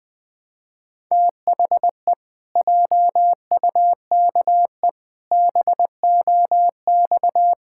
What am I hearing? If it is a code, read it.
Morse code, heard as a keyed tone, about 20 wpm, 700 Hz. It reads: THE JUKE BOX
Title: Pangram Challenge v2 - 20wpm